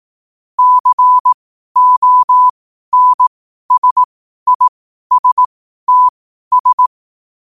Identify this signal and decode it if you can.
Morse code, heard as a keyed tone, about 18 wpm, spaced as for 12 wpm, 1000 Hz. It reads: CONSISTS
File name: qotdfiles / 12wpm